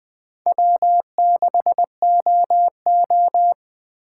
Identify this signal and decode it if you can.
Morse code, heard as a keyed tone, about 21 wpm, 700 Hz. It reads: W6OO
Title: USA Call Signs - 3x Character Spacing - 20wpm